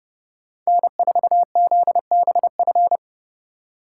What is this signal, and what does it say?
Morse code, heard as a keyed tone, about 39 wpm, 700 Hz. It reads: N4ZBF